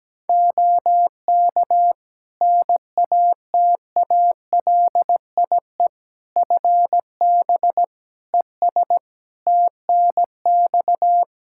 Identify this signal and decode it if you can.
Morse code, heard as a keyed tone, about 17 wpm, 700 Hz. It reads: OK NATALIE FB ES TNX
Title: Practice QSO - Exchange 2 - Info - 17wpm